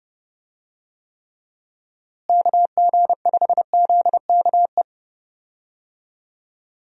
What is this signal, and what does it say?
Morse code, heard as a keyed tone, about 31 wpm, 700 Hz. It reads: KG5ZKE
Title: USA Call Signs - 6 Characters - Repeated 2x - 30wpm